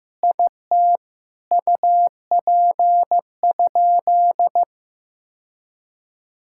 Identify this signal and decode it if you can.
Morse code, heard as a keyed tone, about 15 wpm, 700 Hz. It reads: IT UP?
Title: Sentences from Top 1600 Words 15wpm